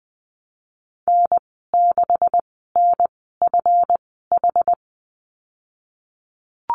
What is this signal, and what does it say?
Morse code, heard as a keyed tone, about 20 wpm, spaced as for 15 wpm, 700 Hz. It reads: N6NFH
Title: 186 Lesson 30g - Letter b - Call Signs 20wpm